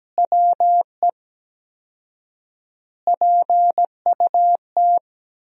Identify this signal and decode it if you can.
Morse code, heard as a keyed tone, about 17 wpm, 700 Hz. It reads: WE PUT
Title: 4x Word-Spacing - Sentences from Top 200 Words - 17wpm